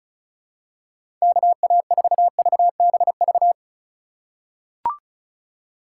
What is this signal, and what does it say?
Morse code, heard as a keyed tone, about 34 wpm, 700 Hz. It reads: KA4VBV
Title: USA Call Signs - 3x Character Spacing - 35wpm